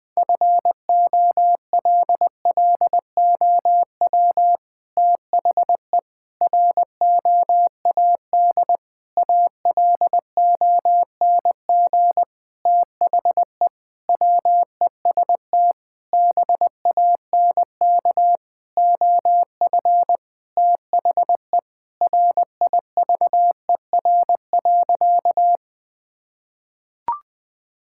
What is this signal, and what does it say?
Morse code, heard as a keyed tone, about 20 wpm, 700 Hz. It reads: FOLLOW THE ROAD ALONG THE WEST BANK OF THE RIVER.
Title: Sentences from Top 1000 Words - Review Part 6 - 20wpm